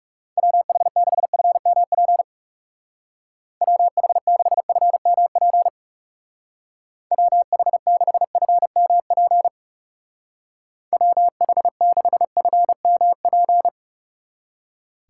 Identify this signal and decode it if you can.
Morse code, heard as a keyed tone, about 38 wpm, 700 Hz. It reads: WH6FMP WH6FMP WH6FMP WH6FMP